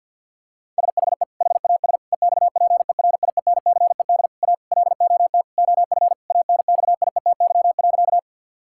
Morse code, heard as a keyed tone, about 50 wpm, 700 Hz. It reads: SHE HAS EXPERIENCED A LOT OF ANXIETY.
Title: Sentences from Top 1600 Words 50wpm